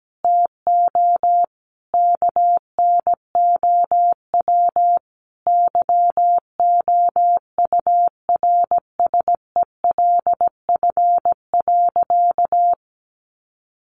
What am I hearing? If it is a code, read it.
Morse code, heard as a keyed tone, about 17 wpm, 700 Hz. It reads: TO KNOW YOURSELF.
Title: Sentences from Top 600 Words 17wpm